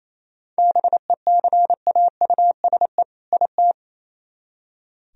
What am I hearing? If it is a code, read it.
Morse code, heard as a keyed tone, about 34 wpm, 700 Hz. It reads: BECAUSE IT
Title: Speed-Racing - Sets of 2 Words 28wpm